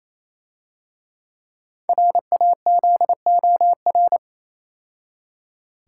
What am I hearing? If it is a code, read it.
Morse code, heard as a keyed tone, about 28 wpm, 700 Hz. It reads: RAZOR
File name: Easy Words - Speed-Racing with Hints - 22wpm